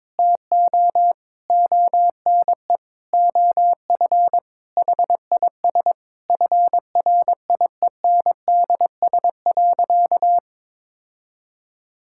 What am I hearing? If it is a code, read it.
Morse code, heard as a keyed tone, about 22 wpm, 700 Hz. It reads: TO ONE OF HIS FRIENDS.
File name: practice / 22wpm / Sentences with 4-Word Phrases - 22wpm